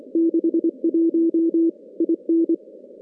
Morse code, heard as a keyed tone, about 25 wpm, 330 Hz. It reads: 61 IN
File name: contests / CW-SS-k5tr- -2100